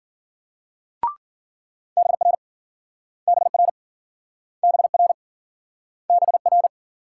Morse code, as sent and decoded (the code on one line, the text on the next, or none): -... .-. / -... .-. / -... .-. / -... .-.
BR BR BR BR